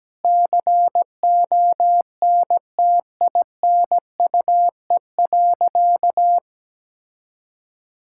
Morse code, heard as a keyed tone, about 17 wpm, 700 Hz. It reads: CONTINUE.
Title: Sentences from Top 1000 Words - Review Part 6 - 17wpm